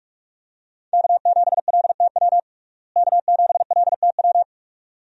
Morse code, heard as a keyed tone, about 44 wpm, 700 Hz. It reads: K7LTW K7LTW
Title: USA Call Signs - 5 Characters - Repeated 2x - 45wpm